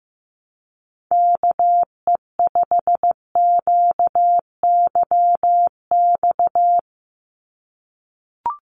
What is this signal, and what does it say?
Morse code, heard as a keyed tone, about 15 wpm, 700 Hz. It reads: KE5QYX
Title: USA Call Signs 15wpm